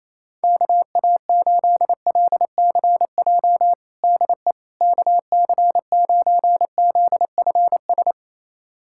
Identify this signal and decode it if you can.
Morse code, heard as a keyed tone, about 27 wpm, 700 Hz. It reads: KA8LCJ DE KC9ZFS
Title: USA Callsign DE Callsign - 6 Characters - 28wpm